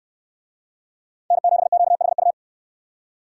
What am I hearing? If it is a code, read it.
Morse code, heard as a keyed tone, about 68 wpm, 700 Hz. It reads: N7QHU